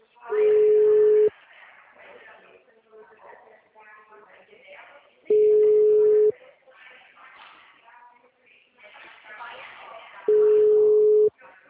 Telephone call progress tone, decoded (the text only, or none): ringback tone